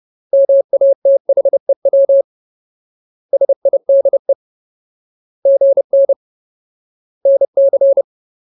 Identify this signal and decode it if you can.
Morse code, heard as a keyed tone, about 30 wpm, 550 Hz. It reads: MATHEW SIDE GN NC
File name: Ultimate Word List / 30WPM Ultimate